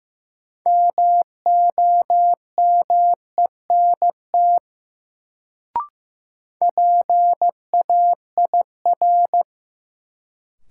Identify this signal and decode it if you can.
Morse code, heard as a keyed tone, about 15 wpm, 700 Hz. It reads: MOMENT PAIR
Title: Sets of 3 Words - New Word Drill - 15wpm